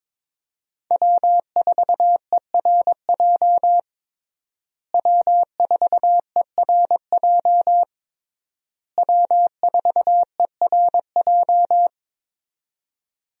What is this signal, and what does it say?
Morse code, heard as a keyed tone, about 22 wpm, 700 Hz. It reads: W4ERJ W4ERJ W4ERJ
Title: USA Call Signs - 5 Characters - Repeated 3x - 22wpm